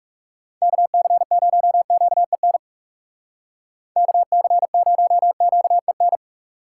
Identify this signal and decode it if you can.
Morse code, heard as a keyed tone, about 39 wpm, 700 Hz. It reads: KC0QEN KC0QEN